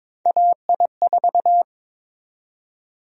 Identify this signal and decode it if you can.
Morse code, heard as a keyed tone, about 22 wpm, 700 Hz. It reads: AI4
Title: USA Call Sign Prefixes - 22wpm